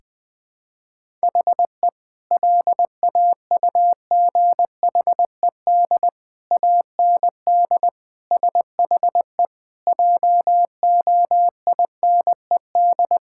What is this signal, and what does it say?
Morse code, heard as a keyed tone, about 21 wpm, 700 Hz. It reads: HE LAUGHED AND SHE JOINED
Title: Sentences with 2-Word Phrases - 20wpm